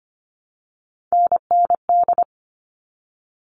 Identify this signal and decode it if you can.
Morse code, heard as a keyed tone, about 25 wpm, 700 Hz. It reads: NND
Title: All Cut Numbers - 3-Digit 25wpm